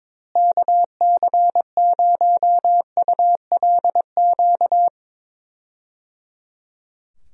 Morse code, heard as a keyed tone, about 22 wpm, 700 Hz. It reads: KC0ULQ
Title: USA Call Signs - 6 Characters - Repeated 3x - 22wpm